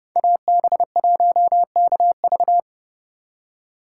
Morse code, heard as a keyed tone, about 37 wpm, 700 Hz. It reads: AB1KV
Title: Speed Racing - USA Call Signs 30wpm